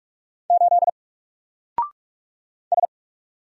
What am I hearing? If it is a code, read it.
Morse code, heard as a keyed tone, about 44 wpm, 700 Hz. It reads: 8 S
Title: Single Letter-Number Alt 45wpm